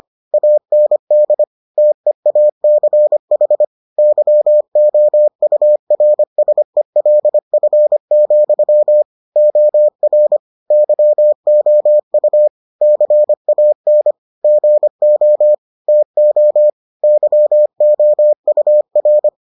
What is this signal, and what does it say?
Morse code, heard as a keyed tone, about 25 wpm, 600 Hz. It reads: AND TEACH YOURSELF, OR YOU CAN GO TO YOUR